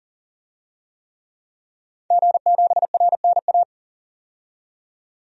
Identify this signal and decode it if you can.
Morse code, heard as a keyed tone, about 40 wpm, 700 Hz. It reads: G7RNA
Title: DXCC-100 Call Signs - 5 Characters - 40wpm